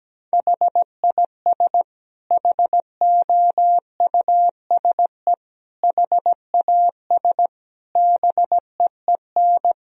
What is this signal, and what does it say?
Morse code, heard as a keyed tone, about 17 wpm, 700 Hz. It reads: HIS HOUSE HAS BEEN
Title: Sentences from Top 1000 Words - Review Part 3 - 17wpm